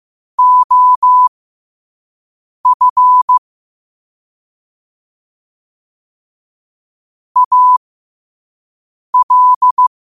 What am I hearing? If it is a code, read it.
Morse code, heard as a keyed tone, about 15 wpm, spaced as for 5 wpm, 1000 Hz. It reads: OF AL
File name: qotdfiles / 5wpm